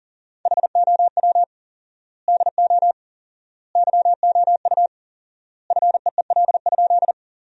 Morse code, heard as a keyed tone, about 40 wpm, 700 Hz. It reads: HOW DO YOU FEEL?
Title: 4x Word-Spacing - Sentences from Top 400 Words - 40wpm